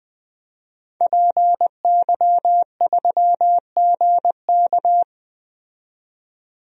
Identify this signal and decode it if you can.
Morse code, heard as a keyed tone, about 20 wpm, 700 Hz. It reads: PY3GK
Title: DXCC-100 Call Signs - 5 Characters - 20wpm